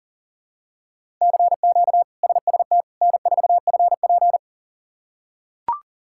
Morse code, heard as a keyed tone, about 37 wpm, 700 Hz. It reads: CQ SST N4FP
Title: K1USN SST QSO Practice - 40wpm